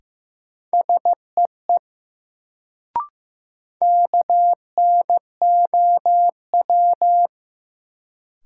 Morse code, heard as a keyed tone, about 15 wpm, 700 Hz. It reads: SEE KNOW
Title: Sentences from Top 100 Words - New Word Drill - 15wpm